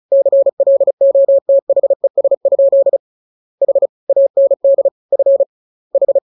Morse code, heard as a keyed tone, about 36 wpm, 550 Hz. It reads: CLOTHES? H AND F H